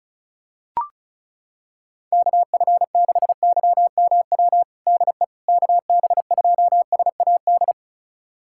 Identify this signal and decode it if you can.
Morse code, heard as a keyed tone, about 36 wpm, 700 Hz. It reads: KF6YMW DE KB2SAD
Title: USA Callsign DE Callsign 35wpm